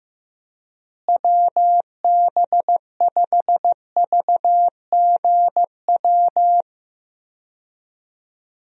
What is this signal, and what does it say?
Morse code, heard as a keyed tone, about 15 wpm, 700 Hz. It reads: WB5VGW
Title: USA Call Signs - 6 Characters - Repeated 3x - 15wpm